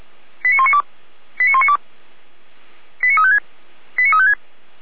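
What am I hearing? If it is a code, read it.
Five-tone selective call (ZVEI), heard as quick runs of five tones, tones of about 70 ms. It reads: 89192 89192 89356 89356
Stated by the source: ZVEI five-tone call 89336 received on SDR.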